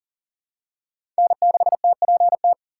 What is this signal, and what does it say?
Morse code, heard as a keyed tone, about 40 wpm, 700 Hz. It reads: N6TPT
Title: USA Call Signs - 5 Characters - 3x Character Spacing - 40wpm